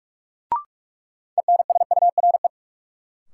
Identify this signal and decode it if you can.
Morse code, heard as a keyed tone, about 47 wpm, 700 Hz. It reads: ENSURE
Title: Sentences from Top 1100 Words - New Word Drill Alt - 45wpm